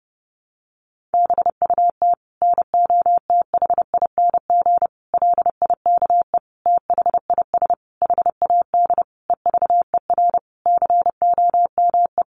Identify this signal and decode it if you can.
Morse code, heard as a keyed tone, about 30 wpm, 700 Hz. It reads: BUT NOTHING LIKE THIS HAD EVER COME
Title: Sentences from Top 1100 Words 30wpm